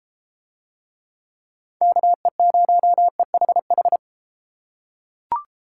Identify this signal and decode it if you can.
Morse code, heard as a keyed tone, about 30 wpm, 700 Hz. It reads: KE0EHH